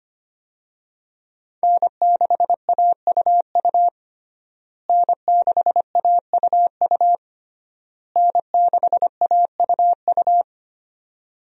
Morse code, heard as a keyed tone, about 25 wpm, 700 Hz. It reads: N6AUU N6AUU N6AUU
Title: USA Call Signs - 5 Characters - Repeated 3x - 25wpm